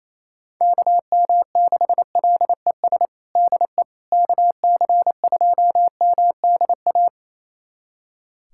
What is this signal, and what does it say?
Morse code, heard as a keyed tone, about 28 wpm, 700 Hz. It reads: KM6LES DE KC2MDA
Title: USA Callsign DE Callsign - 6 Characters - 28wpm